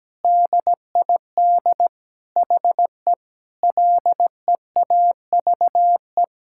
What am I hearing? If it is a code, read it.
Morse code, heard as a keyed tone, about 17 wpm, 700 Hz. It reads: DID HE LEAVE
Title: Sentences from Top 1600 Words 17wpm